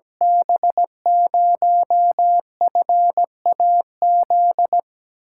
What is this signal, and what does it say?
Morse code, heard as a keyed tone, about 17 wpm, 700 Hz. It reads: B0FAZ
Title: USA Call Signs - 8x Character Spacing - 17wpm